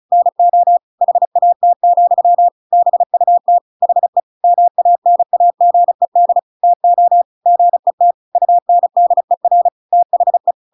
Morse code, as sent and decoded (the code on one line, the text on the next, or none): -. --- / .... .- - --..-- / -... ..- - / .... / -- .- -. .- --. -.. / - --- / --. - / ..- -. -.. .-. / - .... .
NO HAT, BUT HE MANAGED TO GET UNDER THE